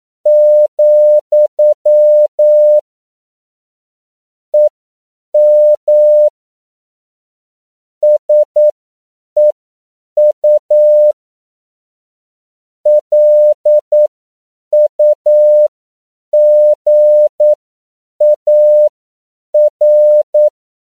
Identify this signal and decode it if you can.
Morse code, heard as a keyed tone, about 9 wpm, spaced as for 7 wpm, 600 Hz. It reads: , EM SEU LUGAR
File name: CW / 10PPM